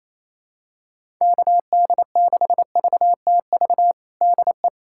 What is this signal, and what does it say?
Morse code, heard as a keyed tone, about 28 wpm, 700 Hz. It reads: KD6VTV DE
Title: USA Callsign DE Callsign 28wpm